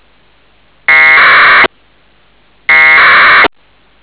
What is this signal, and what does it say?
Packet radio (AX.25, as in APRS).